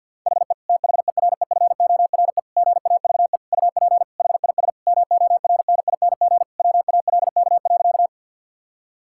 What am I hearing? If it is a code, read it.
Morse code, heard as a keyed tone, about 51 wpm, 700 Hz. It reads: HE THEREFORE GAVE UP HIS MORNING WALK.
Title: Sentences from Top 1000 Words - Review Part 5 - 50wpm